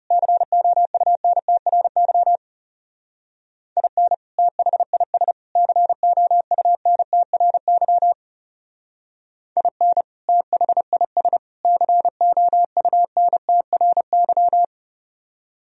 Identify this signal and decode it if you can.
Morse code, heard as a keyed tone, about 34 wpm, 700 Hz. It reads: COUNTRY IN THIS COUNTRY IN THIS COUNTRY
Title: Speed-Racing - Sets of 3 Words 30wpm